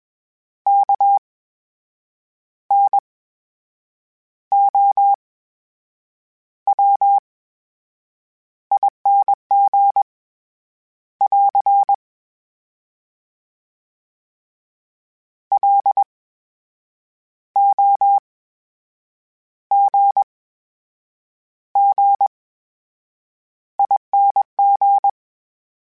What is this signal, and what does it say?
Morse code, heard as a keyed tone, about 21 wpm, spaced as for 7 wpm, 800 Hz. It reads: KNOWING <AR> LOGGING
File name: Special INGs-mp3s / Special INGs-mp3s / VerySlowToFaster Ing @20